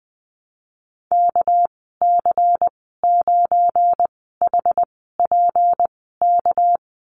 Morse code, heard as a keyed tone, about 19 wpm, 700 Hz. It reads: K C 9 H P K
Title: 238 Lesson 38g - Letter q - Call Signs 20wpm